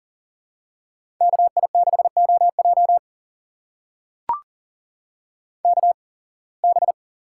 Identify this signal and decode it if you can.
Morse code, heard as a keyed tone, about 40 wpm, spaced as for 26 wpm, 700 Hz. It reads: KI6OJ KB
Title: USA Call Signs - 5 Characters - 8x Character Spacing - 40wpm